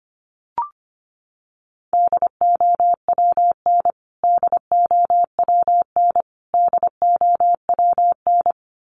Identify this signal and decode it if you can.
Morse code, heard as a keyed tone, about 25 wpm, 700 Hz. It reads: DOWN DOWN DOWN
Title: Top 500 Words Repeated 25wpm